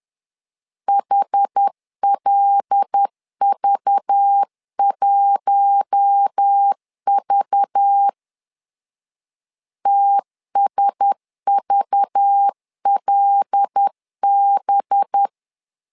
Morse code, heard as a keyed tone, about 10 wpm, 800 Hz. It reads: HLV1V TSVLB
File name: LEZIONI / 5 LBVC / 5 LBV 10wpm 2248